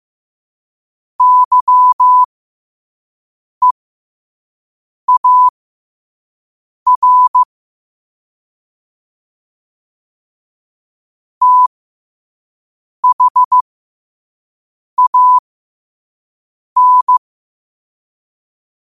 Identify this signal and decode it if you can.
Morse code, heard as a keyed tone, about 15 wpm, spaced as for 5 wpm, 1000 Hz. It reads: YEAR THAN
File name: qotdfiles / 5wpm